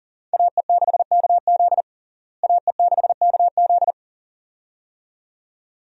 Morse code, heard as a keyed tone, about 40 wpm, 700 Hz. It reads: AE6KZ AE6KZ
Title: USA Call Signs - 5 Characters - Repeated 2x - 40wpm